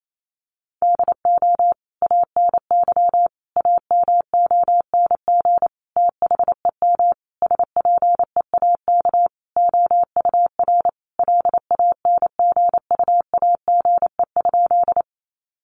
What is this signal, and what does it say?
Morse code, heard as a keyed tone, about 28 wpm, 700 Hz. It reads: DO ANY AMONG THEM SPEAK OUR LANGUAGE?
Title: Sentences from Top 1100 Words 28wpm